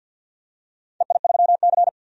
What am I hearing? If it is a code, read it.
Morse code, heard as a keyed tone, about 51 wpm, 700 Hz. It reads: EI3C
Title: DXCC-100 Call Signs - 4 Characters - 50wpm